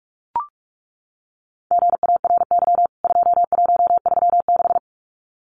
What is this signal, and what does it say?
Morse code, heard as a keyed tone, about 44 wpm, 700 Hz. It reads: GARY 2136
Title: CWT Exchange 45wpm